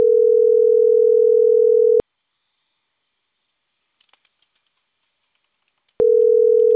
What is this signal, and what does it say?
Ringback tone (North America), 440 + 480 Hz.